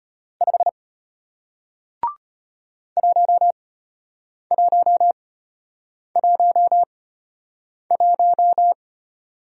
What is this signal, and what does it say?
Morse code, heard as a keyed tone, about 34 wpm, spaced as for 8 wpm, 700 Hz. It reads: 5 1111